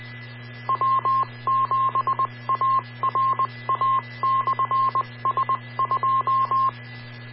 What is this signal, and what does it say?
Morse code, heard as a keyed tone, about 20 wpm, 1060 Hz, with radio noise. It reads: W7ARA/S2